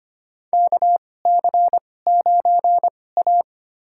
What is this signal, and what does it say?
Morse code, heard as a keyed tone, about 25 wpm, spaced as for 18 wpm, 700 Hz. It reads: KC9A
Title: USA Call Signs - 4 Characters - 2x Character Spacing - 25wpm